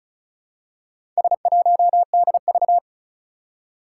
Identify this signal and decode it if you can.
Morse code, heard as a keyed tone, about 36 wpm, 700 Hz. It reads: S1DV
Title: Four Challenging Letter-Number Combinations 35wpm